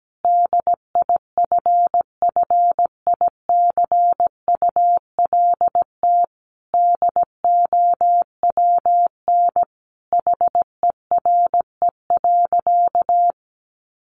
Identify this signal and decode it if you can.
Morse code, heard as a keyed tone, about 17 wpm, 700 Hz. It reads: DIFFICULT DOWN HERE.
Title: Sentences from Top 1100 Words 17wpm